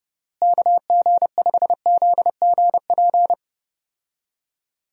Morse code, heard as a keyed tone, about 30 wpm, 700 Hz. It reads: KG5ZGP
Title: USA Call Signs - 6 Characters - Repeated 3x - 30wpm